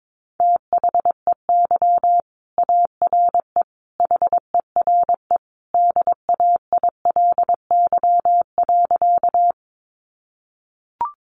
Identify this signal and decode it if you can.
Morse code, heard as a keyed tone, about 22 wpm, 700 Hz. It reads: THEY ARE HERE DAILY.
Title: Sentences from Top 1100 Words 22wpm